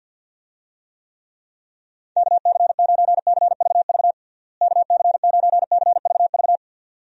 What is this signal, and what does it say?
Morse code, heard as a keyed tone, about 51 wpm, 700 Hz. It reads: KC9CVV KC9CVV
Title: USA Call Signs - 6 Characters - Repeated 3x - 50wpm